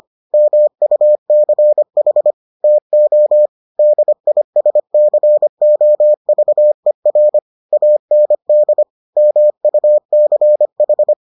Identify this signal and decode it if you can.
Morse code, heard as a keyed tone, about 25 wpm, 600 Hz. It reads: MUCH TO DISCOVER AND MUCH